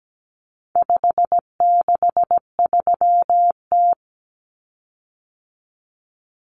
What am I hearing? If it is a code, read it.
Morse code, heard as a keyed tone, about 17 wpm, 700 Hz. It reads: 563T
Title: Common Cut Numbers - 4-Digit 17wpm